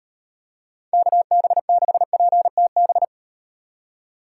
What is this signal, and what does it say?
Morse code, heard as a keyed tone, about 38 wpm, 700 Hz. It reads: KB6PTB